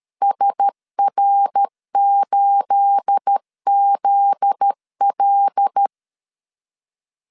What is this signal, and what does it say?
Morse code, heard as a keyed tone, about 13 wpm, 800 Hz. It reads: SR8ZL
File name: CW / DET / 1 9 1500C S B 12wpm 3819